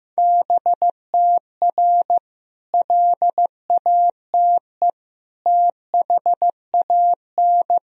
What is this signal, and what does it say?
Morse code, heard as a keyed tone, about 15 wpm, 700 Hz. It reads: BTR LATE THAN
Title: Sentences from Top 400 Words and Abbreviations 15wpm